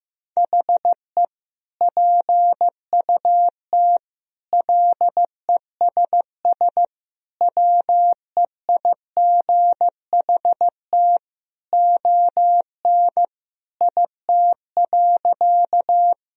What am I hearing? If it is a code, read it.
Morse code, heard as a keyed tone, about 16 wpm, 700 Hz. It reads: HE PUT LESS WEIGHT ON IT.
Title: Sentences from Top 1000 Words - Review Part 3 - 15wpm